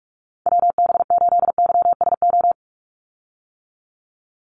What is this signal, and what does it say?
Morse code, heard as a keyed tone, about 46 wpm, 690 Hz. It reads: WB8YSO
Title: USA Call Signs 45wpm